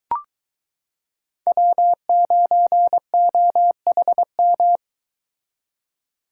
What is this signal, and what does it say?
Morse code, heard as a keyed tone, about 22 wpm, 700 Hz. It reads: W9OHM